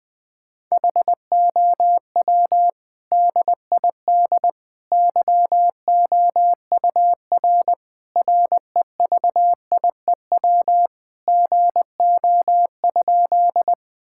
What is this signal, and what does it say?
Morse code, heard as a keyed tone, about 21 wpm, 700 Hz. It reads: HOW DID YOUR REVIEW GO?
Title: Sentences from Top 1000 Words - Review Part 6 - 20wpm